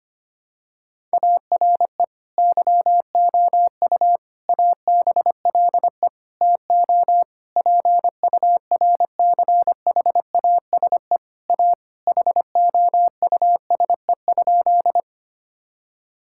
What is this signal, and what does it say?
Morse code, heard as a keyed tone, about 25 wpm, 700 Hz. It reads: ARE YOU ABLE TO PURCHASE A HOUSE?
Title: Sentences from Top 1000 Words - Review Part 5 - 25wpm